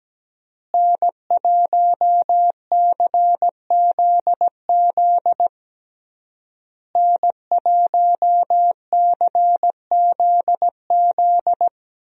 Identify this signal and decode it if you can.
Morse code, heard as a keyed tone, about 17 wpm, 700 Hz. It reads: N1CZZ N1CZZ
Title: USA Call Signs - 5 Characters - Repeated 3x - 17wpm